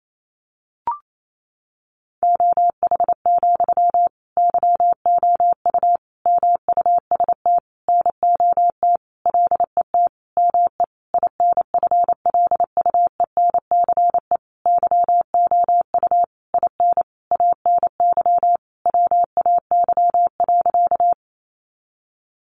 Morse code, heard as a keyed tone, about 28 wpm, 700 Hz. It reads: OH, YOU MUST NOT LET ME INFLUENCE YOU IN ANY WAY.
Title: Sentences from Top 1100 Words 28wpm